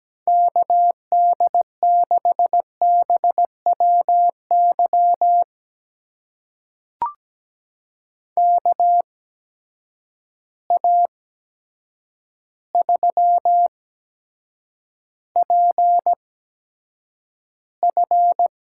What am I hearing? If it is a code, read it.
Morse code, heard as a keyed tone, about 17 wpm, spaced as for 8 wpm, 700 Hz. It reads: KD6BWY KA3PF
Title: USA Call Signs - 6 Characters - 8x Character Spacing - 17wpm